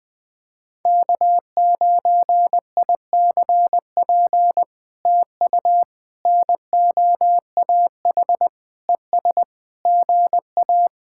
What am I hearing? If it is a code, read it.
Morse code, heard as a keyed tone, about 20 wpm, 700 Hz. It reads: K9ICP TU NOAH ES GA
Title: POTA QSO Practice - 20wpm